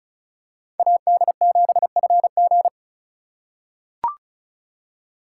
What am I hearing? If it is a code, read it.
Morse code, heard as a keyed tone, about 34 wpm, 700 Hz. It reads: AD7FG
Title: USA Call Signs - 3x Character Spacing - 35wpm